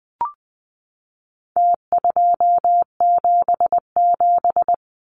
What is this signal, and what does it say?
Morse code, heard as a keyed tone, about 20 wpm, 700 Hz. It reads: T277
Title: Numbers - Contest Style - 4-Digit 20wpm